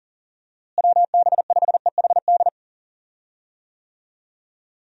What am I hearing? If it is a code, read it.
Morse code, heard as a keyed tone, about 40 wpm, 700 Hz. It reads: WB5EHD
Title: USA Call Signs - 6 Characters - Repeated 2x - 40wpm